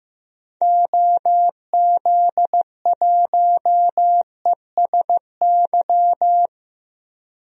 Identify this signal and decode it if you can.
Morse code, heard as a keyed tone, about 15 wpm, 700 Hz. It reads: OZ1ESY
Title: DXCC-100 Call Signs - 6 Characters - Repeated 2x - 15wpm